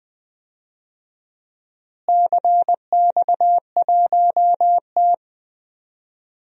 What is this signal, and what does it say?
Morse code, heard as a keyed tone, about 20 wpm, 700 Hz. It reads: CX1T
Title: DXCC-100 Call Signs - 4 Characters - Repeated 2x - 20wpm